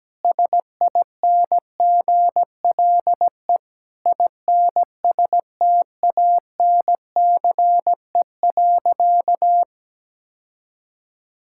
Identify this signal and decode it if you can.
Morse code, heard as a keyed tone, about 17 wpm, 700 Hz. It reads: SINGLE INSTANCE.
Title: Sentences from Top 1000 Words - Review Part 5 - 17wpm